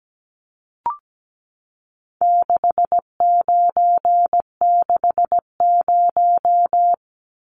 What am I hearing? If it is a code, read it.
Morse code, heard as a keyed tone, about 17 wpm, 700 Hz. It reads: 6960
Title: Numbers - 4-Digit 17wpm